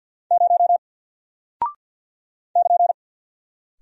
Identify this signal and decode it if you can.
Morse code, heard as a keyed tone, about 51 wpm, 700 Hz. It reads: 0 <KN>
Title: Single Character (Letter:Number:Punctuation:Prosign) Alt 50wpm